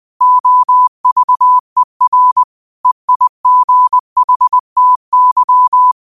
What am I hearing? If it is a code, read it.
Morse code, heard as a keyed tone, about 20 wpm, 1000 Hz. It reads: OVER EIGHTY